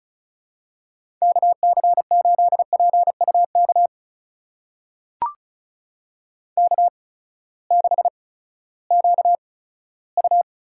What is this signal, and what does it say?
Morse code, heard as a keyed tone, about 34 wpm, spaced as for 11 wpm, 700 Hz. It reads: KC8PU KK6QU
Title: USA Call Signs - 8x Character Spacing - 35wpm